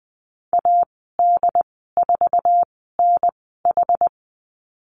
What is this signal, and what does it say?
Morse code, heard as a keyed tone, about 20 wpm, spaced as for 15 wpm, 700 Hz. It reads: AD4NH
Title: 088 Lesson 15g - Number 5 - Call Signs 20wpm